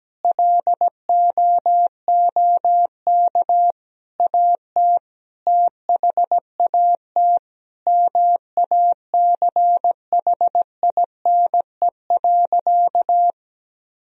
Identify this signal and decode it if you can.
Morse code, heard as a keyed tone, about 17 wpm, 700 Hz. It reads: LOOK AT THAT MACHINE.
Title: Sentences from Top 1000 Words - Review Part 2 - 17wpm